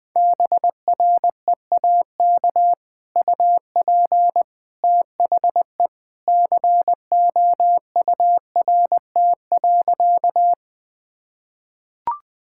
Morse code, heard as a keyed tone, about 20 wpm, 700 Hz. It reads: BREAK UP THE COURT.
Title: Sentences from Top 1600 Words 20wpm